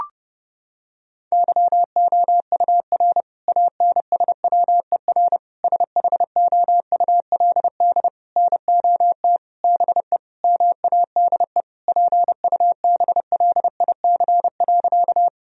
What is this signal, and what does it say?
Morse code, heard as a keyed tone, about 30 wpm, 700 Hz. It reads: YOUR ANSWER SHOULD NOT BE MADE PUBLIC.
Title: Sentences from Top 1000 Words - Review Part 3 - 30wpm